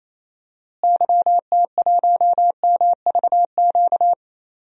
Morse code, heard as a keyed tone, about 28 wpm, 700 Hz. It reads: YT1MVQ
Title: DXCC-100 Call Signs - 6 Characters - Repeated 3x - 28wpm